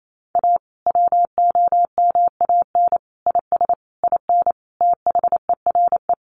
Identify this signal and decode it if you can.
Morse code, heard as a keyed tone, about 28 wpm, 700 Hz. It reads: A WOMAN IS IN THERE
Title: Sentences from Top 1400 Words 28wpm